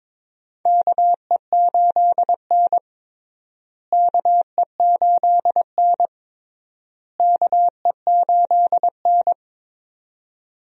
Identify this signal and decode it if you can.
Morse code, heard as a keyed tone, about 22 wpm, 700 Hz. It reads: KE8N KE8N KE8N
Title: USA Call Signs - 4 Characters - Repeated 3x - 22wpm